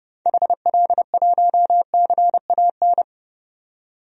Morse code, heard as a keyed tone, about 31 wpm, 700 Hz. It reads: HL1CAN
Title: DXCC-100 Call Signs - 6 Characters - 30wpm